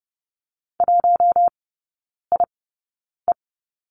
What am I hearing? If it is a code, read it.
Morse code, heard as a keyed tone, about 30 wpm, spaced as for 9 wpm, 700 Hz. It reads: 1IE
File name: 081 Lesson 14e - Number 2 - Three Characters 30wpm